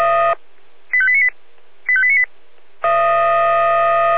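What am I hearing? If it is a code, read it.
Five-tone selective call (ZVEI), heard as quick runs of five tones, tones of about 70 ms. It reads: 85898 85898
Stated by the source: ZVEI five-tone call received on SDR.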